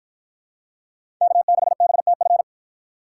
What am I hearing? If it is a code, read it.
Morse code, heard as a keyed tone, about 51 wpm, 700 Hz. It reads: K6BTF